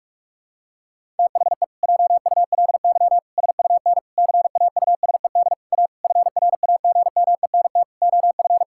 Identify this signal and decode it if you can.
Morse code, heard as a keyed tone, about 46 wpm, 700 Hz. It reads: THE JULY SUN CAUSED A FRAGMENT OF
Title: Pangram Challenge v2 - 45wpm